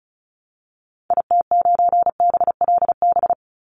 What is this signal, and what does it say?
Morse code, heard as a keyed tone, about 34 wpm, 700 Hz. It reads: IT9BLB
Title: Call Signs Repeated 3x - 35wpm